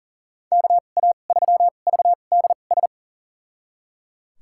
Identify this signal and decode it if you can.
Morse code, heard as a keyed tone, about 37 wpm, spaced as for 28 wpm, 700 Hz. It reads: KA3VDS
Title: USA Call Signs - 6 Characters - 2x Character Spacing - 40wpm